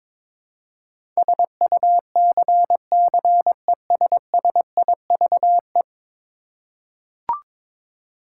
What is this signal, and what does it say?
Morse code, heard as a keyed tone, about 22 wpm, 700 Hz. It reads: SUCCESSIVE
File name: practice / 22wpm / 10 Letter Words - First 3 Letters Given - 22wpm